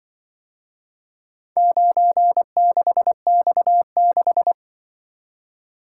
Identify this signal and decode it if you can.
Morse code, heard as a keyed tone, about 24 wpm, 700 Hz. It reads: 96X6
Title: Speed-Racing - Four Letter-Number Combinations - 20wpm